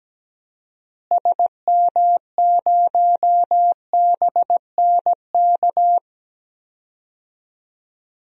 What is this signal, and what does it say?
Morse code, heard as a keyed tone, about 17 wpm, 700 Hz. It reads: SM0BNK
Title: DXCC-100 Call Signs - 6 Characters - Repeated 3x - 17wpm